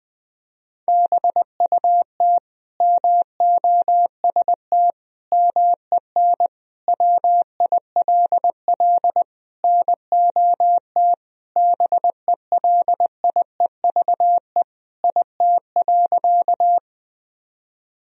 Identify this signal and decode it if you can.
Morse code, heard as a keyed tone, about 20 wpm, 700 Hz. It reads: BUT MOST MEN WILL NOT BELIEVE IT.
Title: Sentences from Top 1000 Words - Review Part 1 - 20wpm